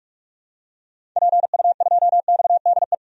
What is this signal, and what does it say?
Morse code, heard as a keyed tone, about 46 wpm, 700 Hz. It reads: PU2XDE
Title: DXCC-100 Call Signs - 6 Characters - 45wpm